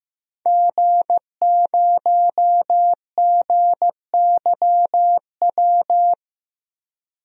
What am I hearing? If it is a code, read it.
Morse code, heard as a keyed tone, about 15 wpm, 700 Hz. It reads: G0GYW
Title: DXCC-100 Call Signs - 5 Characters - 15wpm